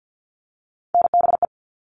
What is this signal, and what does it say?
Morse code, heard as a keyed tone, about 50 wpm, 700 Hz. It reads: NBE
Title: All Cut Numbers - 3-Digit 50wpm